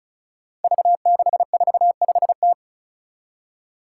Morse code, heard as a keyed tone, about 36 wpm, 700 Hz. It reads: V645T
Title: Five Challenging Letter-Number Combinations 35wpm